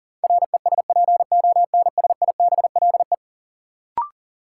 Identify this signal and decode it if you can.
Morse code, heard as a keyed tone, about 40 wpm, 700 Hz. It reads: RESPONSIBLE
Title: Sentences from Top 1000 Words - Review Part 7 - New Word Drill - 40wpm